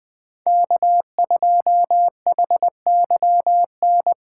Morse code, heard as a keyed tone, about 20 wpm, 700 Hz. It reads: K2HYN
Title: USA Call Signs - 5 Characters - 2x Character Spacing - 20wpm